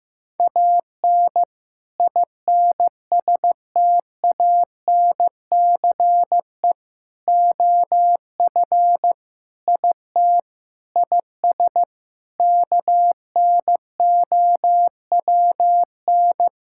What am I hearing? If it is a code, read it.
Morse code, heard as a keyed tone, about 16 wpm, 700 Hz. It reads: AN INSTANCE OF IT IS KNOWN
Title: Sentences from Top 1000 Words - Review Part 5 - 15wpm